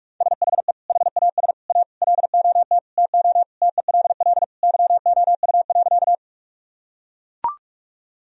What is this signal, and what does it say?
Morse code, heard as a keyed tone, about 41 wpm, 700 Hz. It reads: SHE HAS A LOT TO TELL YOU.
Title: Sentences from Top 1000 Words - Review Part 3 - 45wpm